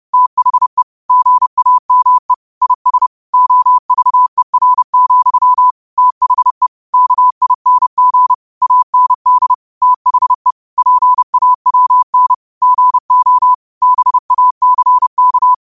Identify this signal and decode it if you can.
Morse code, heard as a keyed tone, about 30 wpm, 1000 Hz. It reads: THE GAME IS OVER, THE KING AND THE PAWN GO BACK